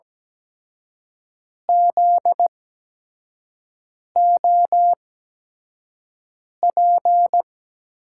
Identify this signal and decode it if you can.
Morse code, heard as a keyed tone, about 17 wpm, spaced as for 5 wpm, 700 Hz. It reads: ZOP
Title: USA Call Signs - 8x Character Spacing - 17wpm